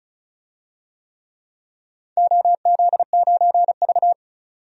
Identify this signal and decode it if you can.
Morse code, heard as a keyed tone, about 36 wpm, 700 Hz. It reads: OZ9V
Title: DXCC-100 Call Signs - 4 Characters - 35wpm